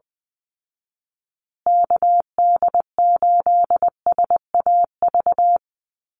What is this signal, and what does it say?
Morse code, heard as a keyed tone, about 20 wpm, 700 Hz. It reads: KD8SAV
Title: Call Signs Repeated 2x - 20wpm